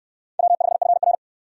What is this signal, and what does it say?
Morse code, heard as a keyed tone, about 69 wpm, 700 Hz. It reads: W5FR